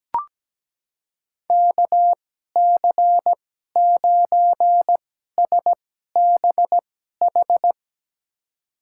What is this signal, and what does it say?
Morse code, heard as a keyed tone, about 17 wpm, spaced as for 12 wpm, 700 Hz. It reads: KC9SBH
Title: USA Call Signs - 6 Characters - 2x Character Spacing - 17wpm